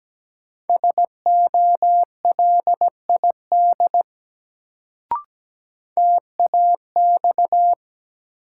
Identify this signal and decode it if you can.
Morse code, heard as a keyed tone, about 17 wpm, 700 Hz. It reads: SOLID TAX
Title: Sentences from Top 1500 Words - New Word Drill Alt - 17wpm